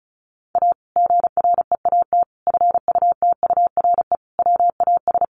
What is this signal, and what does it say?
Morse code, heard as a keyed tone, about 34 wpm, 700 Hz. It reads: A GREAT FUTURE WAS
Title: Sentences from Top 700 Words 35wpm